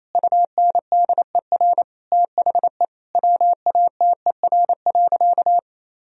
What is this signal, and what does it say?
Morse code, heard as a keyed tone, about 28 wpm, 700 Hz. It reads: UNDER THE WATER.
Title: Sentences from Top 1000 Words - Review Part 2 - 28wpm